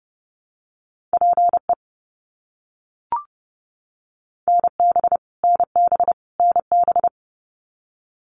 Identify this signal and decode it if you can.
Morse code, heard as a keyed tone, about 30 wpm, 700 Hz. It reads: PE NB NB NB
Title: Canadian Provinces and Territories - Repeated 3x - 30wpm